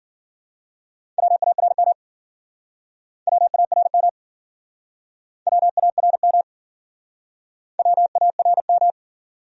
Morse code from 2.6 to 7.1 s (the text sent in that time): WARM WARM